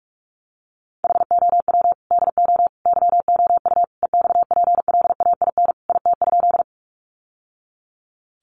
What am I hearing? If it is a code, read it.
Morse code, heard as a keyed tone, about 46 wpm, 700 Hz. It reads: HOW DO YOU EXPLAIN IT?
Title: Sentences from Top 1100 Words 45wpm